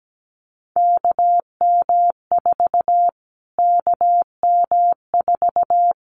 Morse code, heard as a keyed tone, about 17 wpm, 700 Hz. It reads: KM4 KM4
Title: Callsign Prefixes for Top 25 DXCC - Repeated 3x - 17wpm